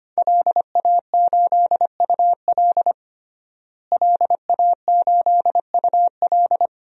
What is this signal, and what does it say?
Morse code, heard as a keyed tone, about 25 wpm, 700 Hz. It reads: LA8UL LA8UL
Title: DXCC-100 Call Signs - 5 Characters - Repeated 2x - 25wpm